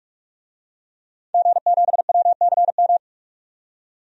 Morse code, heard as a keyed tone, about 42 wpm, 700 Hz. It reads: G7WCM
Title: DXCC-100 Call Signs - 5 Characters - 45wpm